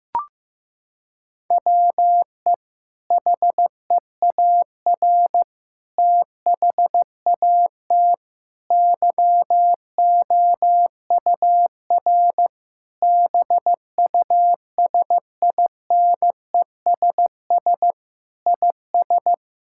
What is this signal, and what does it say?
Morse code, heard as a keyed tone, about 15 wpm, 700 Hz. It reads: WE HEAR THAT YOUR BUSINESS IS
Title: Sentences from Top 1000 Words - Review Part 5 - 15wpm